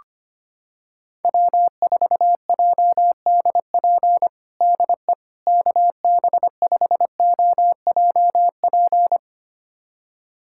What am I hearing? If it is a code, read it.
Morse code, heard as a keyed tone, about 25 wpm, 700 Hz. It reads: W4JDP DE KB5OJP
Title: USA Callsign DE Callsign 25wpm